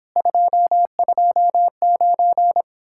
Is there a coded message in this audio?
Yes: Morse code from 0.2 to 2.6 s, heard as a keyed tone, about 29 wpm, 700 Hz. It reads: 229